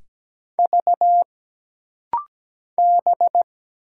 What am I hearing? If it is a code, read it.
Morse code, heard as a keyed tone, about 17 wpm, 700 Hz. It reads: V B
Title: Single Letters - Rapid-Fire 17wpm